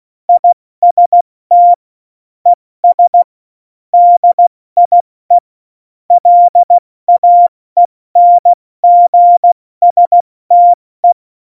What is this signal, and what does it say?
Morse code, heard as a keyed tone, about 16 wpm, 700 Hz. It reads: IST ES DIE LAENGSTE
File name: Morsekurs / Witze / Witz 09. April 2019, 70 F80